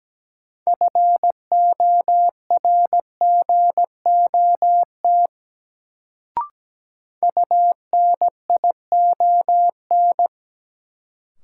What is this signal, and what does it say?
Morse code, heard as a keyed tone, about 17 wpm, 700 Hz. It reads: FORGOT UNION
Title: Sentences from Top 1400 Words - New Word Drill Alt - 17wpm